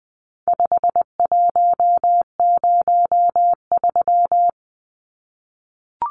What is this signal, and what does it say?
Morse code, heard as a keyed tone, about 19 wpm, 700 Hz. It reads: 5103
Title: Numbers - 4-Digit 20wpm